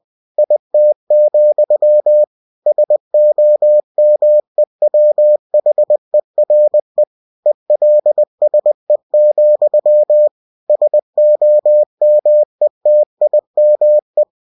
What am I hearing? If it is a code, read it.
Morse code, heard as a keyed tone, about 20 wpm, 600 Hz. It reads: IT, SOMEWHERE ELSE, SOMETIME
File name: cw-books / winnie-the-pooh / winnie-20wpm-page-0031